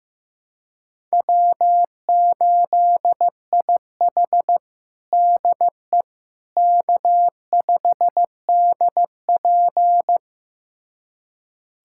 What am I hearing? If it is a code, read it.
Morse code, heard as a keyed tone, about 15 wpm, 700 Hz. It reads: W8IH DE K5DP
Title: USA Callsign DE Callsign - 4 Characters - 15wpm